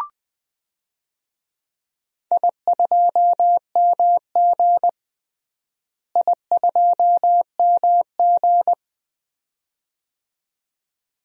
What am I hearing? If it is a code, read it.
Morse code, heard as a keyed tone, about 19 wpm, 700 Hz. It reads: I2MG I2MG
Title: DXCC-100 Call Signs - 4 Characters - Repeated 2x - 20wpm